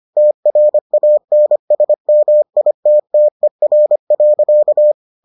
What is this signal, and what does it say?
Morse code, heard as a keyed tone, about 25 wpm, 600 Hz. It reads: TRANSMITTER.